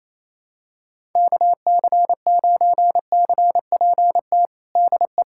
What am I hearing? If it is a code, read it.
Morse code, heard as a keyed tone, about 28 wpm, 700 Hz. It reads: KC9CPT DE
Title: USA Callsign DE Callsign - 6 Characters - 28wpm